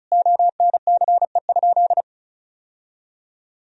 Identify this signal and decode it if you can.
Morse code, heard as a keyed tone, about 36 wpm, 700 Hz. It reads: ONCE?
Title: Sentences from Top 1000 Words - Review Part 1 - 35wpm